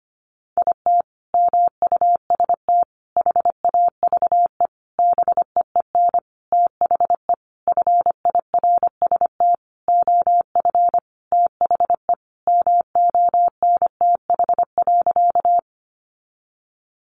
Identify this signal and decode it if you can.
Morse code, heard as a keyed tone, about 25 wpm, 700 Hz. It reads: IT MUST HAVE BEEN THE FIRST OF THE MONTH.
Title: Sentences from Top 800 Words 25wpm